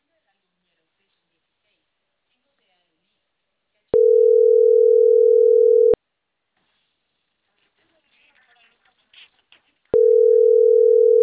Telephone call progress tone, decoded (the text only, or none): ringback tone